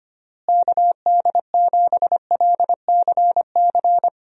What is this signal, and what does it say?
Morse code, heard as a keyed tone, about 25 wpm, 700 Hz. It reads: KD7LCC
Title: USA Call Signs - 6 Characters - Repeated 2x - 25wpm